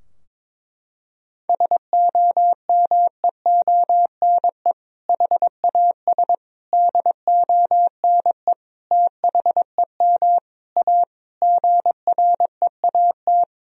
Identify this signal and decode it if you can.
Morse code, heard as a keyed tone, about 22 wpm, 700 Hz. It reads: SOMEONE HAS DONE THEM A GREAT
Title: Sentences from Top 1000 Words - Review Part 4 - 22wpm